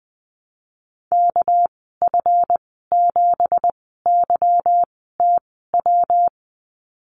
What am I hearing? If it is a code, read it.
Morse code, heard as a keyed tone, about 20 wpm, spaced as for 14 wpm, 700 Hz. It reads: KF7YTW
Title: 245 Lesson 39g - Letter z - Call Signs 20wpm